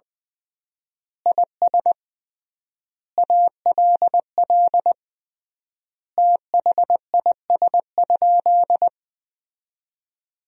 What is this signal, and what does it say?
Morse code, heard as a keyed tone, about 20 wpm, 700 Hz. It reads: IS ALL THIS?
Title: 3x Word-Spacing - Sentences from Top 100 Words - 20wpm